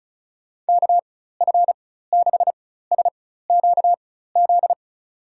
Morse code, heard as a keyed tone, about 36 wpm, spaced as for 16 wpm, 700 Hz. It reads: KF6SQZ
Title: USA Call Signs - 6 Characters - 4x Character Spacing - 35wpm